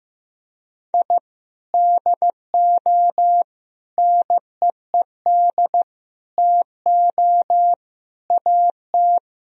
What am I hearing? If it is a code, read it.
Morse code, heard as a keyed tone, about 15 wpm, 700 Hz. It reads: I DO NEED TO AT
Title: Sentences from Top 1000 Words - Review Part 1 - 15wpm